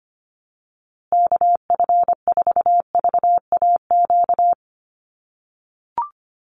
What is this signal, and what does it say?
Morse code, heard as a keyed tone, about 25 wpm, 700 Hz. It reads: KF4VAQ
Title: USA Call Signs Repeated 3x 25wpm